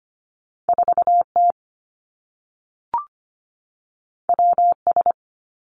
Morse code, heard as a keyed tone, about 25 wpm, 700 Hz. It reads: 4 T W S
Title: Two Challenging Letter-Number Combinations 25wpm